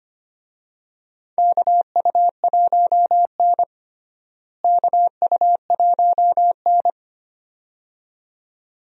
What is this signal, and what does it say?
Morse code, heard as a keyed tone, about 25 wpm, 700 Hz. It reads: KU1N KU1N
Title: USA Call Signs - 4 Characters - Repeated 2x - 25wpm